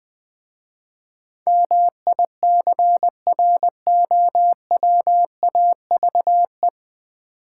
Morse code, heard as a keyed tone, about 20 wpm, 700 Hz. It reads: MICROWAVE
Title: Easy Words - Speed-Racing with Hints - 20wpm